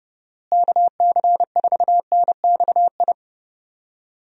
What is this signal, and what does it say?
Morse code, heard as a keyed tone, about 32 wpm, 700 Hz. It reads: KC4NXI
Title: USA Call Signs - 6 Characters - Repeated 2x - 30wpm